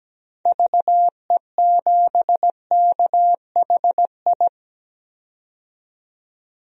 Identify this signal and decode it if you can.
Morse code, heard as a keyed tone, about 17 wpm, 700 Hz. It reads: VE7KHI
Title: DXCC-100 Call Signs - 6 Characters - 17wpm